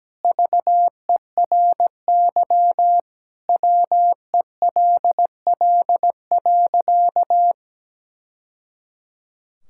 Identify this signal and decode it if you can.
Morse code, heard as a keyed tone, about 17 wpm, 700 Hz. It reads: VERY WELL.
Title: Sentences from Top 1000 Words - Review Part 5 - 17wpm